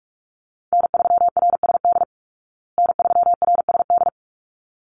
Morse code, heard as a keyed tone, about 44 wpm, 700 Hz. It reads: N3RSD N3RSD
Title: Call Signs Repeated 3x - 45wpm